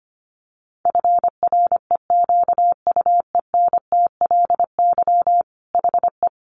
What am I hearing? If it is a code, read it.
Morse code, heard as a keyed tone, about 25 wpm, 700 Hz. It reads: FREQUENTLY HE
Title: Sentences from Top 1300 Words 25wpm